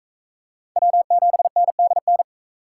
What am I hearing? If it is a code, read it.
Morse code, heard as a keyed tone, about 36 wpm, 700 Hz. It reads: W7NDN